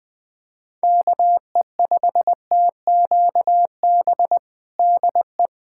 Morse code, heard as a keyed tone, about 20 wpm, 700 Hz. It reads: KE5TQB DE
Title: USA Callsign DE Callsign - 6 Characters - 20wpm